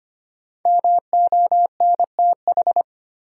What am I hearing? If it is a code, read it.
Morse code, heard as a keyed tone, about 25 wpm, 700 Hz. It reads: MONTH
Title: Sentences from Top 800 Words - New Word Drill - 25wpm